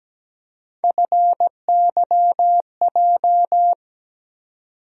Morse code, heard as a keyed tone, about 17 wpm, 700 Hz. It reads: FYJ
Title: USA Call Sign Postfixes - 3 Letters 17wpm